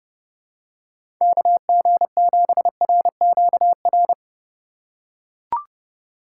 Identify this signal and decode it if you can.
Morse code, heard as a keyed tone, about 30 wpm, 700 Hz. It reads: KG7RQR
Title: USA Call Signs - 6 Characters - 3x Character Spacing - 30wpm